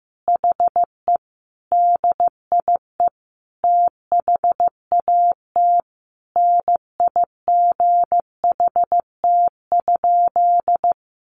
Morse code, heard as a keyed tone, about 15 wpm, 700 Hz. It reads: HE DIE THAT NIGHT?
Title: Sentences from Top 1300 Words 15wpm